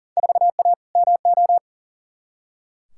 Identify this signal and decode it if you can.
Morse code, heard as a keyed tone, about 40 wpm, 700 Hz. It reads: 4A MO
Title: Field Day Exchanges - 40wpm